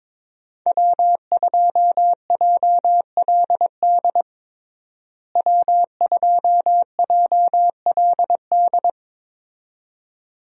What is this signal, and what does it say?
Morse code, heard as a keyed tone, about 22 wpm, 700 Hz. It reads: W2JLD W2JLD
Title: USA Call Signs - 5 Characters - Repeated 2x - 22wpm